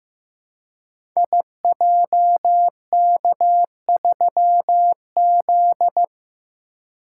Morse code, heard as a keyed tone, about 15 wpm, 700 Hz. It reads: IJK3Z
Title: Code Groups - 5 Letter:Numbers - 15wpm